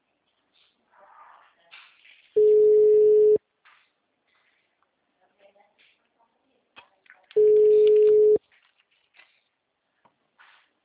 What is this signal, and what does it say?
Ringback tone (Europe, 425 Hz), with speech.
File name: out- -2013- - - .15245